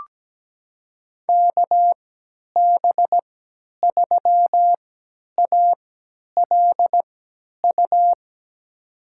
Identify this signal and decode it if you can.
Morse code, heard as a keyed tone, about 17 wpm, spaced as for 10 wpm, 700 Hz. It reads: KB3ALU
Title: USA Call Signs - 6 Characters - 3x Character Spacing - 17wpm